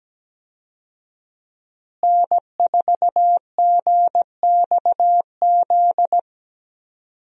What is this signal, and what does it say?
Morse code, heard as a keyed tone, about 17 wpm, 700 Hz. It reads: N4GXZ
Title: USA Call Signs - 5 Characters - Repeated 3x - 17wpm